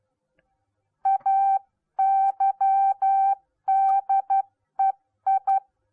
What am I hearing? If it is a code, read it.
Morse code, heard as a keyed tone, about 12 wpm, 780 Hz. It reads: AYDEI